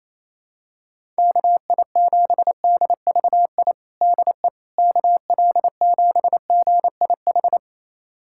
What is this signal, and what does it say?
Morse code, heard as a keyed tone, about 28 wpm, 700 Hz. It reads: KI7DVI DE KL7GIH
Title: USA Callsign DE Callsign - 6 Characters - 28wpm